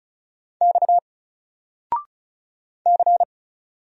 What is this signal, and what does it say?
Morse code, heard as a keyed tone, about 34 wpm, 700 Hz. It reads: X C
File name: Single-Letter-Number-35wpm